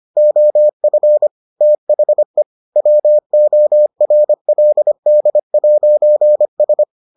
Morse code, heard as a keyed tone, about 25 wpm, 600 Hz. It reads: OF THE WORLD'S